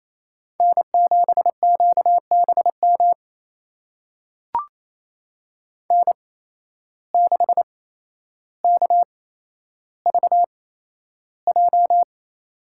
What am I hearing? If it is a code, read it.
Morse code, heard as a keyed tone, about 28 wpm, spaced as for 12 wpm, 700 Hz. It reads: N7QBM N6KVJ
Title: USA Call Signs - 5 Characters - 8x Character Spacing - 28wpm